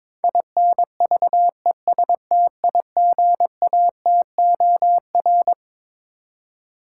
Morse code, heard as a keyed tone, about 22 wpm, 700 Hz. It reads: INVESTIGATOR
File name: Twelve Letter Words - 22wpm